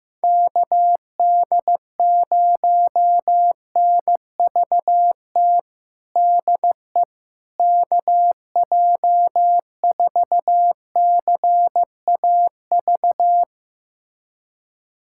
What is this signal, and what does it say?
Morse code, heard as a keyed tone, about 14 wpm, 700 Hz. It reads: KD0NVT DE KJ4CAV
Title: USA Callsign DE Callsign 15wpm